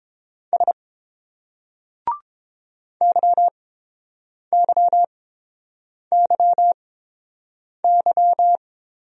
Morse code, heard as a keyed tone, about 30 wpm, spaced as for 7 wpm, 700 Hz. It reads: S YYYY